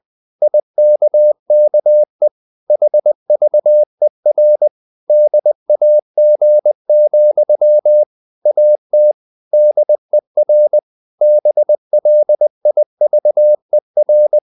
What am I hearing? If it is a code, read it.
Morse code, heard as a keyed tone, about 20 wpm, 600 Hz. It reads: IKKE HVER DAG, AT DER BLIVER
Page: Ferie 20 wpm